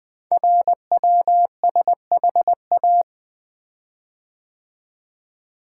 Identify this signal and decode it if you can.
Morse code, heard as a keyed tone, about 20 wpm, 700 Hz. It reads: RWSHA
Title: Code Groups - 5 Letters - 20wpm